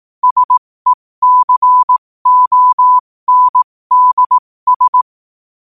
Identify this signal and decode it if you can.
Morse code, heard as a keyed tone, about 18 wpm, spaced as for 15 wpm, 1000 Hz. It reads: SECONDS
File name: qotdfiles / 15wpm